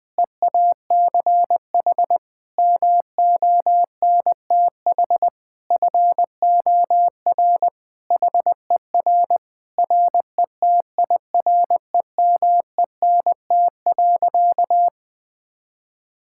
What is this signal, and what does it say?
Morse code, heard as a keyed tone, about 20 wpm, 700 Hz. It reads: EACH MONTH FOR HER RETIREMENT.
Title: Difficult Sentences with Hints - 20wpm